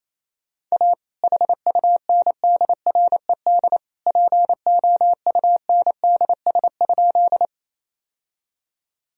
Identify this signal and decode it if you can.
Morse code, heard as a keyed tone, about 28 wpm, 700 Hz. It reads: A HUNDRED POUNDS?
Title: Sentences from Top 1000 Words - Review Part 2 - 28wpm